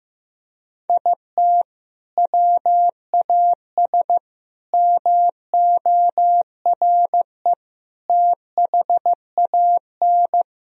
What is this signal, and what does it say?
Morse code, heard as a keyed tone, about 15 wpm, 700 Hz. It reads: IT WAS MORE THAN
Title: Sentences from Top 1000 Words - Review Part 3 - 15wpm